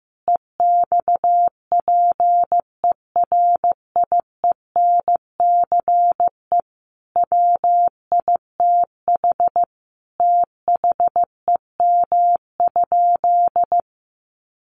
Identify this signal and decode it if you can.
Morse code, heard as a keyed tone, about 15 wpm, 700 Hz. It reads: EXPERIENCE WITH THEM?
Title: Sentences from Top 600 Words 15wpm